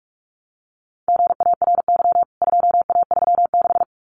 Morse code, heard as a keyed tone, about 44 wpm, 700 Hz. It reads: GARY 2A36
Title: CWT Exchange 45wpm